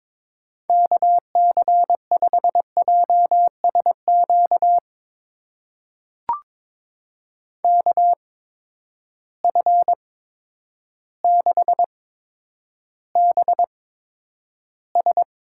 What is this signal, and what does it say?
Morse code, heard as a keyed tone, about 22 wpm, spaced as for 10 wpm, 700 Hz. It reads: KC5JSQ KF6BS
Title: USA Call Signs - 6 Characters - 8x Character Spacing - 22wpm